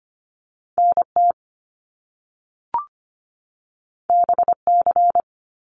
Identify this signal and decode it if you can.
Morse code, heard as a keyed tone, about 25 wpm, 700 Hz. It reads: NT BC
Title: Canadian Provinces and Territories 25wpm